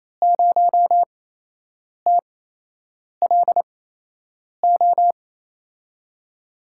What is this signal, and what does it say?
Morse code, heard as a keyed tone, about 28 wpm, spaced as for 8 wpm, 700 Hz. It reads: 0TLO
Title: USA Call Signs - 5 Characters - 8x Character Spacing - 28wpm